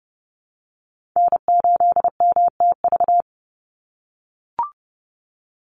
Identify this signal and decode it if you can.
Morse code, heard as a keyed tone, about 31 wpm, 700 Hz. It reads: N8MTV
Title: Call Signs - 30wpm